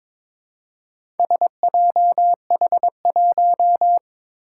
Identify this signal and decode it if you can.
Morse code, heard as a keyed tone, about 22 wpm, 700 Hz. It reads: SJH1
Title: Four Challenging Letter-Number Combinations 22wpm